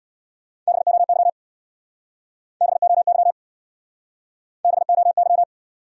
Morse code, heard as a keyed tone, about 66 wpm, 700 Hz. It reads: BOY BOY BOY